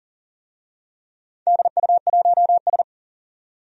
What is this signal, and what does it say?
Morse code, heard as a keyed tone, about 40 wpm, 700 Hz. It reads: DU1S
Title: DXCC-100 Call Signs - 4 Characters - 40wpm